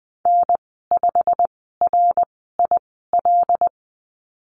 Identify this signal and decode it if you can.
Morse code, heard as a keyed tone, about 20 wpm, spaced as for 15 wpm, 700 Hz. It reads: N5RIL
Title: 088 Lesson 15g - Number 5 - Call Signs 20wpm